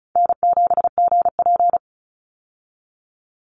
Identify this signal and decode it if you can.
Morse code, heard as a keyed tone, about 36 wpm, 700 Hz. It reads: N7GP
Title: USA Call Signs 35wpm